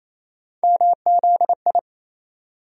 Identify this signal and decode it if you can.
Morse code, heard as a keyed tone, about 27 wpm, 700 Hz. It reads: MZI
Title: USA Call Sign Postfixes - 3 Letters 28wpm